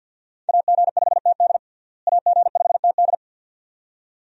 Morse code, heard as a keyed tone, about 47 wpm, 700 Hz. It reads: AG5TD AG5TD
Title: USA Call Signs - 5 Characters - Repeated 2x - 50wpm